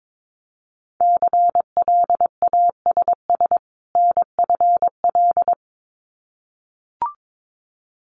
Morse code, heard as a keyed tone, about 22 wpm, 700 Hz. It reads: CLASS NFL
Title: Winter Field Day Exchanges 22wpm